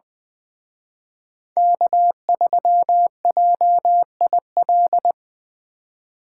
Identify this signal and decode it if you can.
Morse code, heard as a keyed tone, about 24 wpm, 700 Hz. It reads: K3JIL
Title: Speed Racing - USA 1x3 Call Signs 20wpm